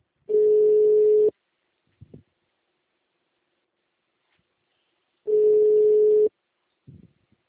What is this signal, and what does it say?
Ringback tone (Europe, 425 Hz).
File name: out- -2008- - - .721